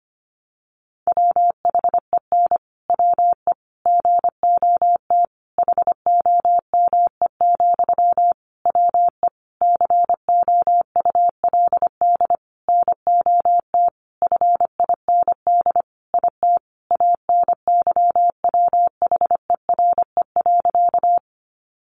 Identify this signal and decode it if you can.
Morse code, heard as a keyed tone, about 25 wpm, 700 Hz. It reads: WHEN WE GOT HOME, WE COULD NOT FIND IT ANYWHERE.
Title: Sentences from Top 1400 Words 25wpm